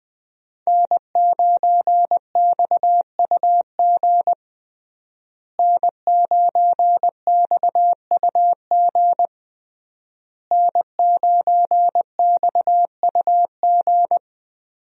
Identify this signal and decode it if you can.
Morse code, heard as a keyed tone, about 20 wpm, 700 Hz. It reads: N9XUG N9XUG N9XUG
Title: USA Call Signs - 5 Characters - Repeated 3x - 20wpm